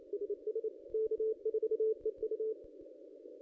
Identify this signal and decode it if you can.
Morse code, heard as a keyed tone, about 30 wpm, 420 Hz. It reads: SS K4EU
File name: contests / ss-cw-k5tr-op-n5rz -2200